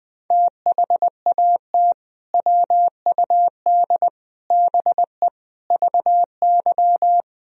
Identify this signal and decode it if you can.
Morse code, heard as a keyed tone, about 21 wpm, 700 Hz. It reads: THAT WUD BE VY
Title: Sentences from Top 200 Words and Abbreviations 20wpm